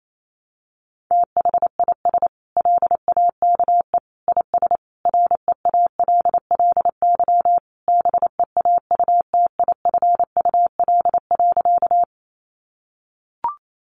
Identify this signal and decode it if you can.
Morse code, heard as a keyed tone, about 28 wpm, 700 Hz. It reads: THIS LAKE IS REALLY BEAUTIFUL.
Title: Sentences from Top 1100 Words 28wpm